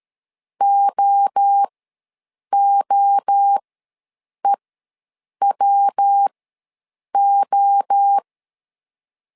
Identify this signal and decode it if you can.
Morse code, heard as a keyed tone, about 13 wpm, spaced as for 7 wpm, 800 Hz. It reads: OOEWO
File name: LEZIONI / 3 MWRO / 3 MWRO ws 12wpm 1914